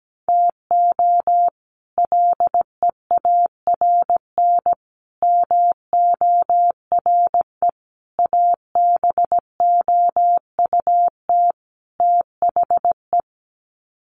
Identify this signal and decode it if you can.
Morse code, heard as a keyed tone, about 17 wpm, 700 Hz. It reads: TO LEARN MORE ABOUT THE
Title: Sets of 5 Words Encore2 17wpm